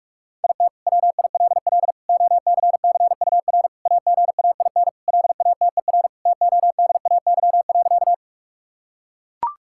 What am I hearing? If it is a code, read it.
Morse code, heard as a keyed tone, about 41 wpm, 700 Hz. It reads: IT WILL OCCUR AGAIN LATER TODAY.
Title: Sentences from Top 1000 Words - Review Part 6 - 45wpm